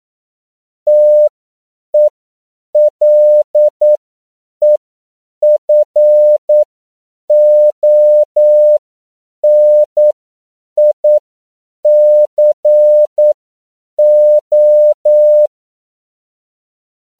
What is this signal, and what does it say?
Morse code, heard as a keyed tone, about 9 wpm, spaced as for 7 wpm, 600 Hz. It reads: TELEFONICO
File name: CW / 10PPM